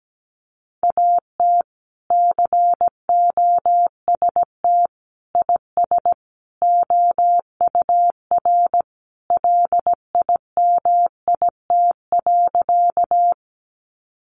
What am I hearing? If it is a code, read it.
Morse code, heard as a keyed tone, about 17 wpm, 700 Hz. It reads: AT COST IS OUR LIMIT.
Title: Sentences from Top 1500 Words 17wpm